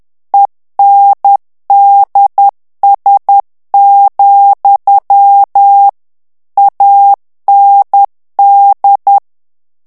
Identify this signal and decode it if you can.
Morse code, heard as a keyed tone, about 10 wpm, 800 Hz. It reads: ENDS, AND
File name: ANGLAIS texte clair 3 vitesse 10 mots minute